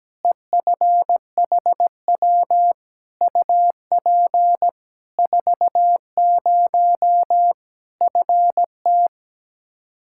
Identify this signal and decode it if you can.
Morse code, heard as a keyed tone, about 17 wpm, 700 Hz. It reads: EFHW UP 40 FT
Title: Practice QSO - Exchange 2 - Info - 17wpm